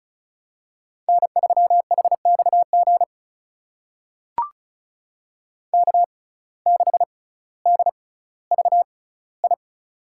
Morse code, heard as a keyed tone, about 36 wpm, spaced as for 18 wpm, 700 Hz. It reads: N3HXG K6DVI
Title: USA Call Signs - 6x Character Spacing - 35wpm